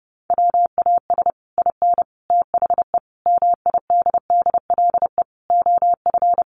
Morse code, heard as a keyed tone, about 30 wpm, 700 Hz. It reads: WAS IN THE MIDDLE OF
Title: Sets of 5 Words Encore 30wpm